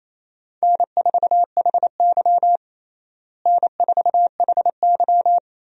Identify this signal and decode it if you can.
Morse code, heard as a keyed tone, about 27 wpm, 700 Hz. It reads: N4HY N4HY
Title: USA Call Signs - 4 Characters - Repeated 2x - 28wpm